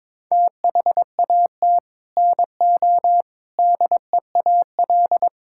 Morse code, heard as a keyed tone, about 22 wpm, 700 Hz. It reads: THAT NO DEAL
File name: Sets of 9 Words - 22wpm